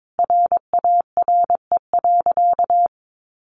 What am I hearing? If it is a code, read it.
Morse code, heard as a keyed tone, about 22 wpm, 700 Hz. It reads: RARE.
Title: Sentences from Top 1100 Words 22wpm